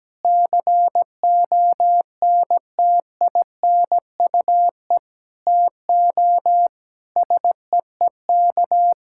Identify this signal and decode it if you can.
Morse code, heard as a keyed tone, about 17 wpm, 700 Hz. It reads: CONTINUE TO SEEK
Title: Sentences from Top 1600 Words 17wpm